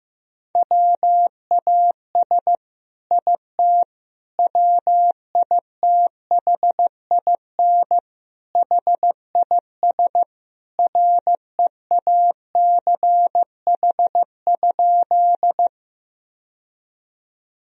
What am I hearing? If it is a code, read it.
Morse code, heard as a keyed tone, about 16 wpm, 700 Hz. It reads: WAS IT WITHIN HIS REACH?
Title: Sentences from Top 1000 Words - Review Part 4 - 15wpm